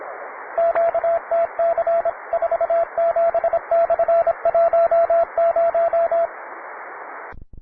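Morse code, heard as a keyed tone, about 25 wpm, 670 Hz, with radio noise. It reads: QTC 47/10
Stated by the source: Morse code QTC list traffic from the Worked All Europe (WAE) contest, advanced level.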